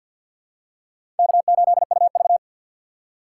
Morse code, heard as a keyed tone, about 47 wpm, 700 Hz. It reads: K8UV
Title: USA Call Signs - 2x Character Spacing - 50wpm